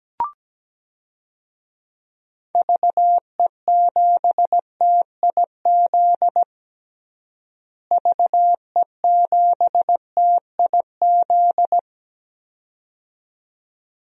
Morse code, heard as a keyed tone, about 17 wpm, 700 Hz. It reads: VE7TIZ VE7TIZ
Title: DXCC-100 Call Signs - 6 Characters - Repeated 2x - 17wpm